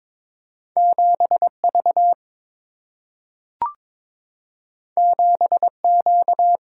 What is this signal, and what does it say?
Morse code, heard as a keyed tone, about 22 wpm, 700 Hz. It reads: 7V 7Q
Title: Single-Letter Single-Number 22wpm